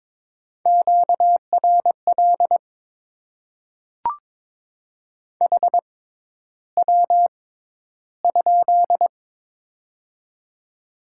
Morse code, heard as a keyed tone, about 23 wpm, spaced as for 11 wpm, 690 Hz. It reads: QRL HW?
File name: Top 20 QSO Elements - 6x Character Spacing - 22wpm